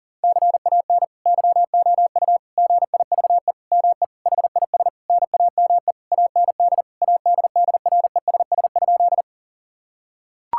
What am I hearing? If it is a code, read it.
Morse code, heard as a keyed tone, about 40 wpm, 700 Hz. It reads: CAN YOU GIVE ME HIS NAME AND ADDRESS?
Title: Sentences from Top 1000 Words - Review Part 6 - 40wpm